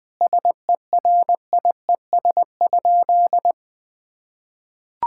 Morse code, heard as a keyed tone, about 21 wpm, 700 Hz. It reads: SERIES?
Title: Sentences from Top 1000 Words - Review Part 5 - 20wpm